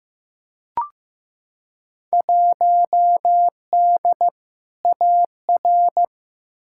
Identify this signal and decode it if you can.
Morse code, heard as a keyed tone, about 15 wpm, 700 Hz. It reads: 1D AR
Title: Field Day Exchanges Encore - 15wpm